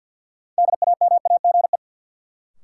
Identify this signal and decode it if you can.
Morse code, heard as a keyed tone, about 50 wpm, 700 Hz. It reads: DAMAGE
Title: Sentences from Top 1100 Words - New Word Drill - 50wpm